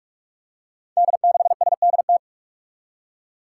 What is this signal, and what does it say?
Morse code, heard as a keyed tone, about 42 wpm, 700 Hz. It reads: D6SDT
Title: Five Challenging Letter-Number Combinations 45wpm